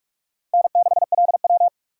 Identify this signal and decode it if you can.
Morse code, heard as a keyed tone, about 40 wpm, 700 Hz. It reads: N6LW